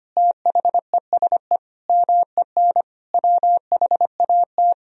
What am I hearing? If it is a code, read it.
Morse code, heard as a keyed tone, about 25 wpm, 700 Hz. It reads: THESE MEN WHAT
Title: Sentences from Top 1000 Words - Review Part 1 - 25wpm